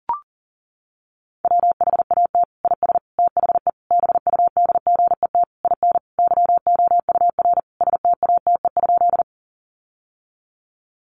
Morse code, heard as a keyed tone, about 40 wpm, 700 Hz. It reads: WHAT IS THE BUDGET IN YOUR STATE?
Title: Sentences from Top 1200 Words 40wpm